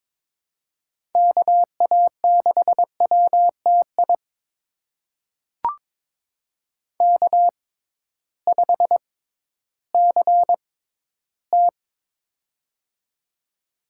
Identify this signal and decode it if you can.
Morse code, heard as a keyed tone, about 22 wpm, spaced as for 13 wpm, 700 Hz. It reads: KA6WTI K5CT
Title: USA Call Signs - 6x Character Spacing - 22wpm